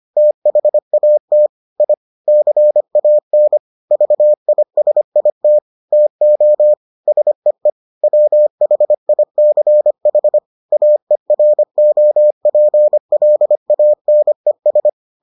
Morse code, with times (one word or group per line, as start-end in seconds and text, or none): 0.2-1.5 THAT
1.8-1.9 I
2.3-3.6 CAN
3.9-5.6 VISIT
5.9-6.7 TO
7.1-7.7 SEE
8.0-10.4 WHICH
10.7-14.9 AEROPLANES